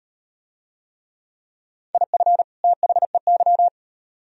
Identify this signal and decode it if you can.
Morse code, heard as a keyed tone, about 38 wpm, 700 Hz. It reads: IF THEY